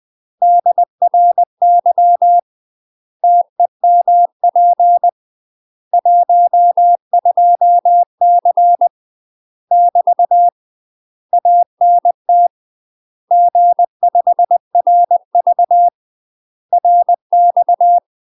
Morse code, heard as a keyed tone, about 20 wpm, 700 Hz. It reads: DRY TEMP 12C <BT> ANT G5RV RX
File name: UK MASTER QSOs TRAINING / 20WPM / UK MASTER QSOs 0016